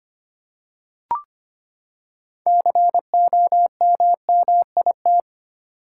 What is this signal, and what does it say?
Morse code, heard as a keyed tone, about 25 wpm, 700 Hz. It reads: COMMIT
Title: Six Letter Words - 25wpm